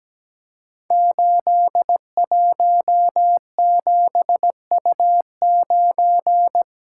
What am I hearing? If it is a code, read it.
Morse code, heard as a keyed tone, about 17 wpm, 700 Hz. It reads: 817U9
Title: Code Groups - 5 Letter:Numbers - 17wpm